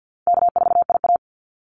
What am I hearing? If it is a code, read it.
Morse code, heard as a keyed tone, about 48 wpm, 700 Hz. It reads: K4IA
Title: Easy Call Signs Repeated 2x - 50wpm